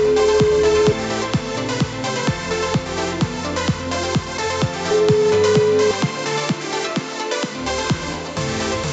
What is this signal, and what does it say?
Ringback tone (Europe, 425 Hz), with noise.